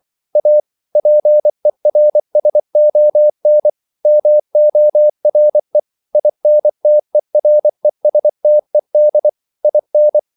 Morse code, heard as a keyed tone, about 24 wpm, 600 Hz. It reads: A PERSON MORE INTERESTED IN
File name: CW / Practice / EnglishQuotations0224